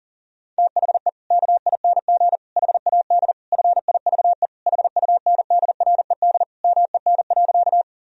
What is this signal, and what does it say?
Morse code, heard as a keyed tone, about 37 wpm, 700 Hz. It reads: THE KING HAD FIVE HUNDRED MEN.
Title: Sentences from Top 1000 Words - Review Part 2 - 40wpm